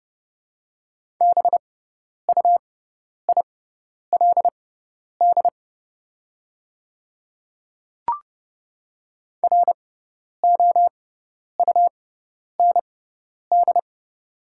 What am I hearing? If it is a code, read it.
Morse code, heard as a keyed tone, about 30 wpm, spaced as for 10 wpm, 700 Hz. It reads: BUILD ROUND
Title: 6x Character Spacing - Top 300 Words - 30wpm